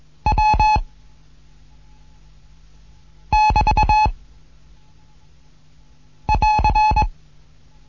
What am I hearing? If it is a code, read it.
Morse code, heard as a keyed tone, about 21 wpm, 860 Hz. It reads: W <BT> <AR>
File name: Morsen / Kurs A 15